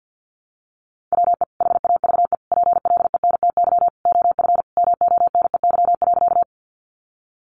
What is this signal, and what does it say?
Morse code, heard as a keyed tone, about 50 wpm, 700 Hz. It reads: WE HAVE PLENTY OF MONEY.
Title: Sentences from Top 1300 Words 50wpm